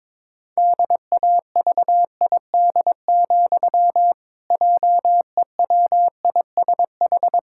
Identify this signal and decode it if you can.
Morse code, heard as a keyed tone, about 22 wpm, 700 Hz. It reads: DAVID, JEWISH
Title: Pangram Challenge v2 - 22wpm